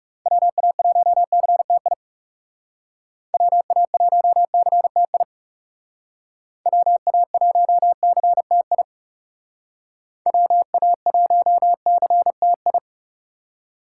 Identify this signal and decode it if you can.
Morse code, heard as a keyed tone, about 35 wpm, 700 Hz. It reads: WA1CTI WA1CTI WA1CTI WA1CTI
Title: Speed Racing - USA Call Signs 30wpm